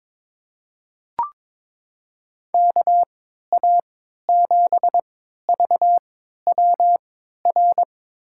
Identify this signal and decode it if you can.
Morse code, heard as a keyed tone, about 22 wpm, spaced as for 13 wpm, 700 Hz. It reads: KA7VWR
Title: USA Call Signs - 6 Characters - 3x Character Spacing - 22wpm